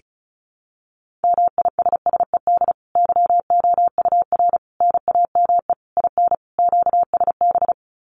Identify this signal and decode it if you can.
Morse code, heard as a keyed tone, about 34 wpm, 700 Hz. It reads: MISSED YOUR NAME IN QSB
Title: Common QSO Phrases 35wpm